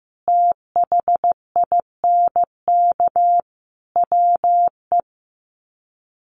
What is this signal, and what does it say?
Morse code, heard as a keyed tone, about 14 wpm, 700 Hz. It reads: THINK WE
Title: Sets of 4 Words 15wpm